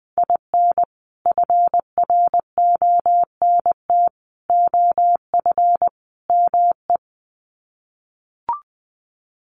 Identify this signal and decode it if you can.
Morse code, heard as a keyed tone, about 20 wpm, 700 Hz. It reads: IN FRONT OF ME
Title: Sets of 4 Words 20wpm